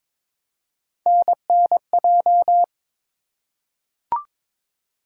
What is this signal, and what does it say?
Morse code, heard as a keyed tone, about 21 wpm, 700 Hz. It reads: NNJ
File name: ARRL Sections v2 - 22wpm